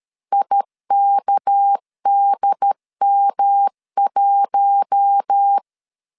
Morse code, heard as a keyed tone, about 13 wpm, 800 Hz. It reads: IKDM1
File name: LEZIONI / 6 PKX0 / 6 PK 12wpm 2006